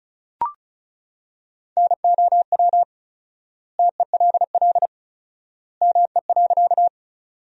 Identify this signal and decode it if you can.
Morse code, heard as a keyed tone, about 34 wpm, 700 Hz. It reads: NOW TELL ME.
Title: 4x Word-Spacing - Sentences from Top 200 Words - 35wpm